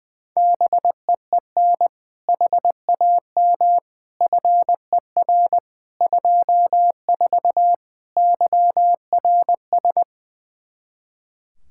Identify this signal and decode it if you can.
Morse code, heard as a keyed tone, about 20 wpm, 700 Hz. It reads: BEEN HAM FER 24 YRS
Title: Practice QSO - Exchange 3 - More Info - 20wpm